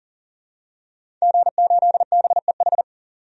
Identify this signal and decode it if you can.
Morse code, heard as a keyed tone, about 40 wpm, 700 Hz. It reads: G8BEH
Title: DXCC-100 Call Signs - 5 Characters - 40wpm